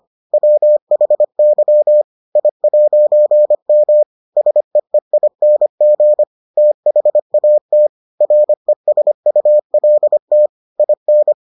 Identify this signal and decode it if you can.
Morse code, heard as a keyed tone, about 25 wpm, 600 Hz. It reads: WHY I'M SEEING THAT RESULT IN